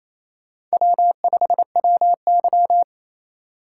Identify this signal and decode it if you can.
Morse code, heard as a keyed tone, about 31 wpm, 700 Hz. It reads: W5WY
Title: Speed Racing - USA 1x2 Call Signs 28wpm